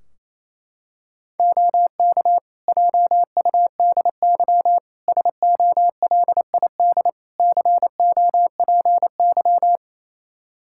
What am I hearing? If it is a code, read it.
Morse code, heard as a keyed tone, about 28 wpm, 700 Hz. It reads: OK JUDY SOLID COPY
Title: Practice QSO - Exchange 3 - More Info - 28wpm